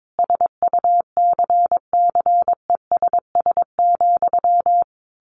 Morse code, heard as a keyed tone, about 22 wpm, 700 Hz. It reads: SUCCESS,
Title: A Sound Of Thunder 22wpm